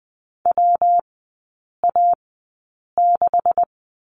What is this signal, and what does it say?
Morse code, heard as a keyed tone, about 20 wpm, spaced as for 8 wpm, 700 Hz. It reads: WA6
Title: 166 Lesson 27e - Number 7 - Three Characters 20wpm